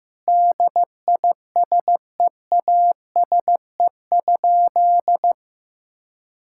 Morse code, heard as a keyed tone, about 15 wpm, 700 Hz. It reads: DISEASE?
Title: Sentences from Top 1000 Words - Review Part 6 - 15wpm